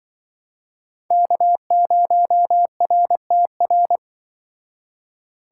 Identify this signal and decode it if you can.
Morse code, heard as a keyed tone, about 25 wpm, 700 Hz. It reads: K0RTR
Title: Speed Racing - USA 1x3 Call Signs 20wpm